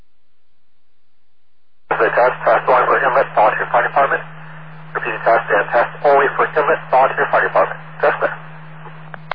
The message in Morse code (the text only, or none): none